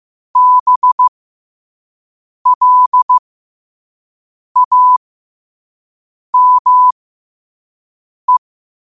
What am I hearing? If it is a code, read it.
Morse code, heard as a keyed tone, about 15 wpm, spaced as for 5 wpm, 1000 Hz. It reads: BLAME
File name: qotdfiles / 5wpm